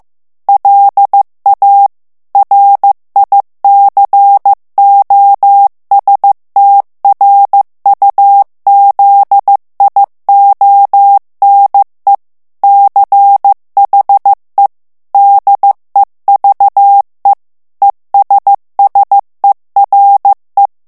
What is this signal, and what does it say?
Morse code, heard as a keyed tone, about 15 wpm, 800 Hz. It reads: LA RICOSTRUZIONE CHE DEVE ESSERE
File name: ITALIEN texte clair 1 vitesse 14 mots minute